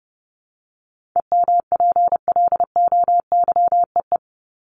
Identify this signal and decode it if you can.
Morse code, heard as a keyed tone, about 30 wpm, 700 Hz. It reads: EMPLOYEE
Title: Top 1501-1750 Words 30wpm